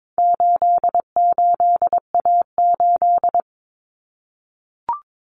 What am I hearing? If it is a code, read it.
Morse code, heard as a keyed tone, about 22 wpm, 700 Hz. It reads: 88A8
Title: Common Cut Numbers - 4-Digit 22wpm